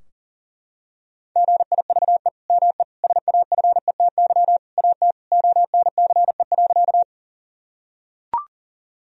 Morse code, heard as a keyed tone, about 40 wpm, 700 Hz. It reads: GIVE ME SAFETY AT ONCE.
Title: Sentences from Top 1000 Words - Review Part 4 - 40wpm